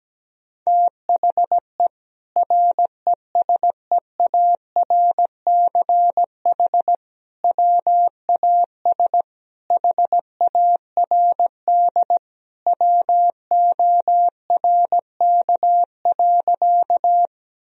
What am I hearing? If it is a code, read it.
Morse code, heard as a keyed tone, about 17 wpm, 700 Hz. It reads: THE RESEARCH WAS HARD WORK.
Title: Sentences from Top 1000 Words - Review Part 4 - 17wpm